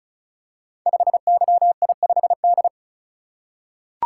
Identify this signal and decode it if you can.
Morse code, heard as a keyed tone, about 36 wpm, 690 Hz. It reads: 5YI5D
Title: Code Groups - 5 Letter:Numbers - 35wpm